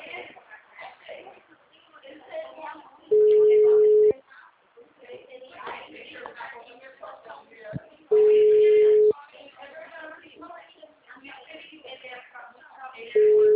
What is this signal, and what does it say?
Ringback tone (Europe, 425 Hz), with speech.